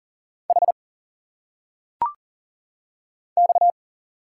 Morse code, heard as a keyed tone, about 40 wpm, 700 Hz. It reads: H X
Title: Single Letters - Phonetic Pronunciation 40wpm